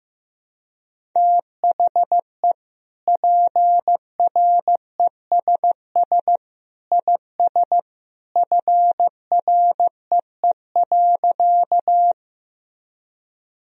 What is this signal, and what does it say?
Morse code, heard as a keyed tone, about 15 wpm, 700 Hz. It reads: THE PRESS IS FREE.
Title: Sentences from Top 1000 Words - Review Part 1 - 15wpm